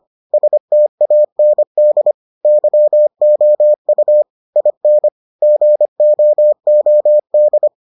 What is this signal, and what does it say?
Morse code, heard as a keyed tone, about 25 wpm, 600 Hz. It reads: STAND YOU IN GOOD